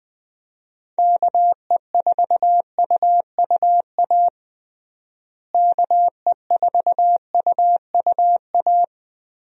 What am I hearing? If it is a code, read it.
Morse code, heard as a keyed tone, about 20 wpm, 700 Hz. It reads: KE4UUA KE4UUA
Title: USA Call Signs - 6 Characters - Repeated 2x - 20wpm